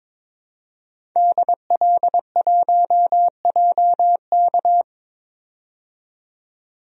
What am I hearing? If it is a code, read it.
Morse code, heard as a keyed tone, about 22 wpm, 700 Hz. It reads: DL1JK
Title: DXCC-100 Call Signs - 5 Characters - 22wpm